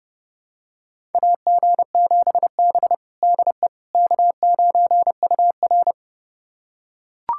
Morse code, heard as a keyed tone, about 30 wpm, 700 Hz. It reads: AG7B DE K9UR
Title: USA Callsign DE Callsign - 4 Characters - 30wpm